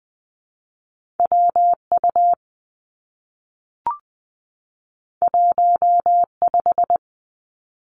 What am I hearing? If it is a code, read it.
Morse code, heard as a keyed tone, about 20 wpm, 700 Hz. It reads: W U 1 5
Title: Two Challenging Letter-Number Combinations 20wpm